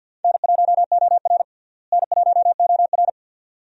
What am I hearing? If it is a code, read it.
Morse code, heard as a keyed tone, about 47 wpm, 700 Hz. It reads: N1OR N1OR
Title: USA Call Signs - 4 Characters - Repeated 2x - 50wpm